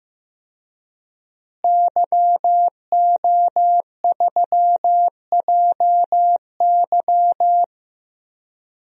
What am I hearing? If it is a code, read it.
Morse code, heard as a keyed tone, about 15 wpm, 700 Hz. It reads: YO3JY
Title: DXCC-100 Call Signs - 5 Characters - Repeated 3x - 15wpm